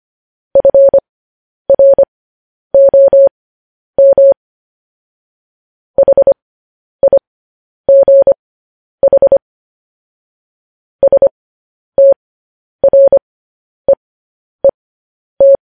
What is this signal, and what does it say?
Morse code, heard as a keyed tone, about 25 wpm, spaced as for 10 wpm, 550 Hz. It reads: FROM HIGH STREET
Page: news-headlines-in-morse-code-at-10-wpm